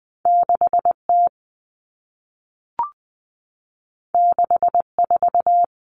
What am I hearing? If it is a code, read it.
Morse code, heard as a keyed tone, about 20 wpm, 700 Hz. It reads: 6T 64
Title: Common Cut Numbers - 2-Digit 20wpm